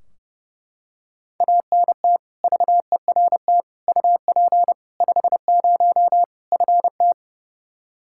Morse code, heard as a keyed tone, about 30 wpm, 700 Hz. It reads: ANT VERT UP 50 FT
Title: Practice QSO - Exchange 2 - Info - 30wpm